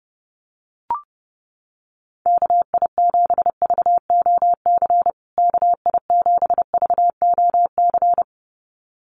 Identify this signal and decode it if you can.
Morse code, heard as a keyed tone, about 30 wpm, 700 Hz. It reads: KI7VOC KI7VOC
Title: USA Call Signs Repeated 2x 30wpm